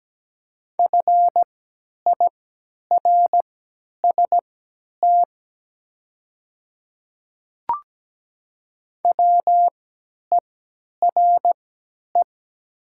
Morse code, heard as a keyed tone, about 17 wpm, spaced as for 10 wpm, 700 Hz. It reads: FIRST WERE
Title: 3x Character Spacing - Top 100 Words - 17wpm